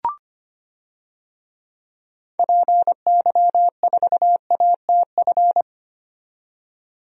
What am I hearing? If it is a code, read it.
Morse code, heard as a keyed tone, about 25 wpm, 690 Hz. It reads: PY4ATF
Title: DXCC-100 Call Signs - 6 Characters - 25wpm